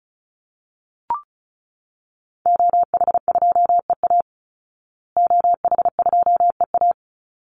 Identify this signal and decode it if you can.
Morse code, heard as a keyed tone, about 34 wpm, 700 Hz. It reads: OH2EA OH2EA
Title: Call Signs Repeated 3x - 35wpm